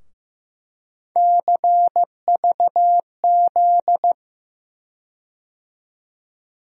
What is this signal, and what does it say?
Morse code, heard as a keyed tone, about 15 wpm, 700 Hz. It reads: CVZ
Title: USA Call Sign Postfixes - 3 Letters 15wpm